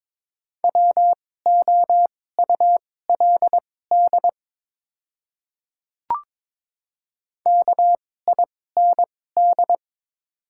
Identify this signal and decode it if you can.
Morse code, heard as a keyed tone, about 22 wpm, spaced as for 16 wpm, 700 Hz. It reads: WOULD KIND
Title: 2x Character Spacing - Top 300 Words - 22wpm